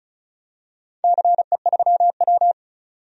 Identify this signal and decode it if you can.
Morse code, heard as a keyed tone, about 36 wpm, 700 Hz. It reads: CE3W
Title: DXCC-100 Call Signs - 4 Characters - 35wpm